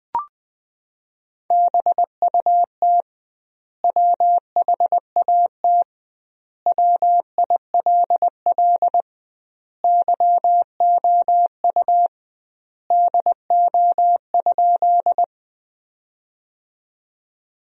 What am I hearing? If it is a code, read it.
Morse code, heard as a keyed tone, about 20 wpm, 700 Hz. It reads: BUT WHAT WILL YOU DO?
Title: 2x Word-Spacing - Sentences from Top 100 Words - 20wpm